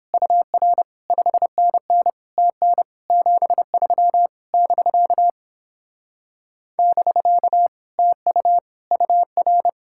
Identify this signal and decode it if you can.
Morse code, heard as a keyed tone, about 29 wpm, 700 Hz. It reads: UR 5NN TN 73 <BK> <BK> TU UR